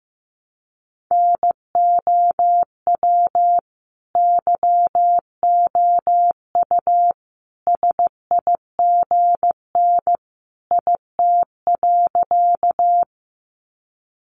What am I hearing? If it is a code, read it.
Morse code, heard as a keyed tone, about 15 wpm, 700 Hz. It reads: NOW YOU SIGN IT.
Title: Sentences from Top 1000 Words 15wpm